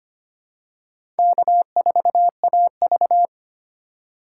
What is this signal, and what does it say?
Morse code, heard as a keyed tone, about 25 wpm, 700 Hz. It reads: K4AV
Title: USA Call Signs - 6x Character Spacing - 25wpm